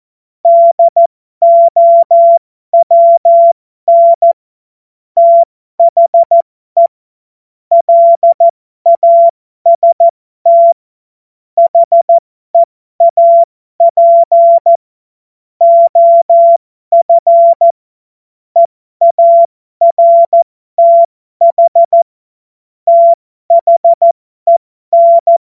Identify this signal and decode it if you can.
Morse code, heard as a keyed tone, about 14 wpm, spaced as for 12 wpm, 680 Hz. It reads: DOWN THE LAST HEAP OF EARTH THEN